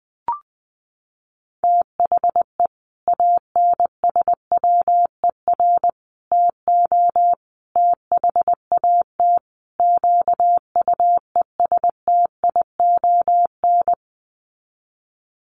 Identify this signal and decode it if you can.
Morse code, heard as a keyed tone, about 20 wpm, 700 Hz. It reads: THE ANSWER TO THAT QUESTION
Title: Sets of 5 Words Encore2 20wpm